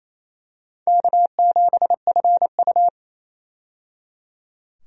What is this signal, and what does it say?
Morse code, heard as a keyed tone, about 28 wpm, 700 Hz. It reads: K7FU
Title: USA Call Signs - 4 Characters - 28wpm